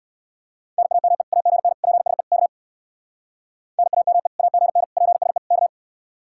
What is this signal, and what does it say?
Morse code, heard as a keyed tone, about 64 wpm, 700 Hz. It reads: NAME AGN PSE K NAME AGN PSE K